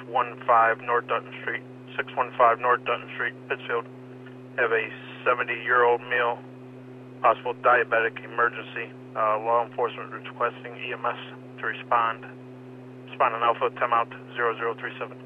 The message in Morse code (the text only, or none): none